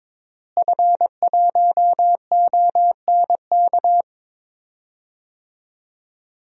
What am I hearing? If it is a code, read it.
Morse code, heard as a keyed tone, about 22 wpm, 700 Hz. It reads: F1ONK
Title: DXCC-100 Call Signs - 5 Characters - Repeated 2x - 22wpm